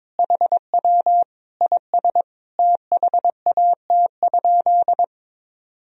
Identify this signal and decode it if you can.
Morse code, heard as a keyed tone, about 21 wpm, 700 Hz. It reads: HW IS THAT?
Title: Sentences from Top 100 Words and Abbreviations 22wpm